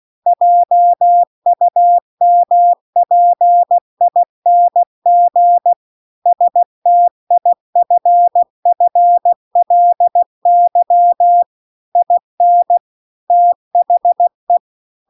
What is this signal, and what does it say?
Morse code, heard as a keyed tone, about 16 wpm, 700 Hz. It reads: JUMPING STIFFLY IN THE
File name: CW-The-Big-Twoheaded-River-2-16WPM